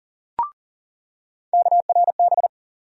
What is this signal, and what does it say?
Morse code, heard as a keyed tone, about 39 wpm, 700 Hz. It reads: KRB
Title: USA Call Sign Postfixes - 3 Letters 40wpm